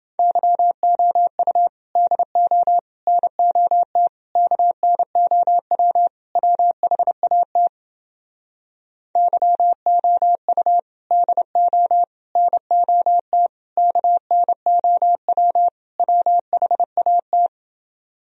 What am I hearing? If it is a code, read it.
Morse code, heard as a keyed tone, about 31 wpm, 700 Hz. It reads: YOU DO NOT KNOW WHAT YOU DO NOT KNOW WHAT